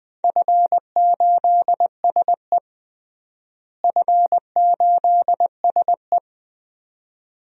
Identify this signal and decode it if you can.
Morse code, heard as a keyed tone, about 21 wpm, 700 Hz. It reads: F8SE F8SE
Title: DXCC-100 Call Signs - 4 Characters - Repeated 2x - 20wpm